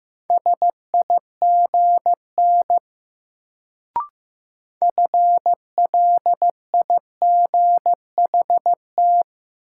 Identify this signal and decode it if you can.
Morse code, heard as a keyed tone, about 15 wpm, 700 Hz. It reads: SIGN FLIGHT
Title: Sentences from Top 1000 Words - New Word Drill - 15wpm